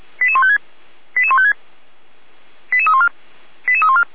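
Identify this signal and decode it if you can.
Five-tone selective call (ZVEI), heard as quick runs of five tones, tones of about 70 ms. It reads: 88156 88156 88314 88314
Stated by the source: ZVEI five-tone selective call 88127 received with an SDR.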